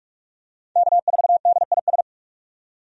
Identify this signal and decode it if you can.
Morse code, heard as a keyed tone, about 46 wpm, 700 Hz. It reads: K4DIS
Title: USA Call Signs - 5 Characters - 45wpm